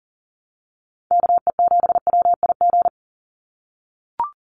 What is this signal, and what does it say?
Morse code, heard as a keyed tone, about 40 wpm, 700 Hz. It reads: KE7WIG
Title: USA Call Signs 40wpm